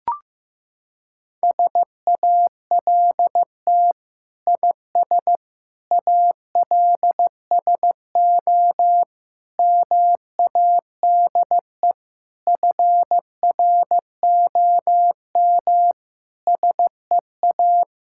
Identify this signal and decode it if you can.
Morse code, heard as a keyed tone, about 15 wpm, 700 Hz. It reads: SALT IS ALSO MADE FROM SEA
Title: Sentences from Top 1000 Words - Review Part 5 - 15wpm